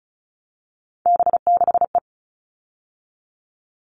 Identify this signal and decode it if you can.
Morse code, heard as a keyed tone, about 34 wpm, 700 Hz. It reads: B6E
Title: All Cut Numbers - 3-Digit 35wpm